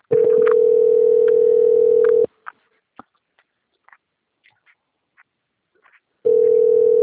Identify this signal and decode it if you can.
Ringback tone (North America), 440 + 480 Hz.
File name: out- - - - - .30607